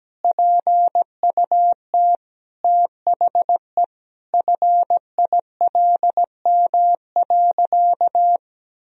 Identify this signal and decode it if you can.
Morse code, heard as a keyed tone, about 17 wpm, 700 Hz. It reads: PUT THE FILM.
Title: Sentences from Top 1000 Words - Review Part 3 - 17wpm